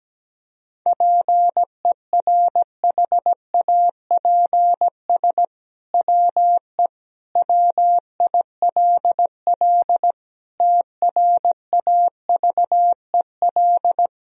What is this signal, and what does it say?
Morse code, heard as a keyed tone, about 17 wpm, 700 Hz. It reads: PERHAPS WE WILL TRAVEL
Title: Sentences from Top 1000 Words - Review Part 2 - 17wpm